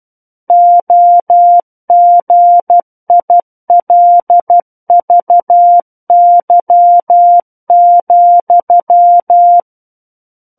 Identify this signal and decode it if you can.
Morse code, heard as a keyed tone, about 12 wpm, 700 Hz. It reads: OGILVY,
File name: Book One / 03-Chapter 3 - 12 WPM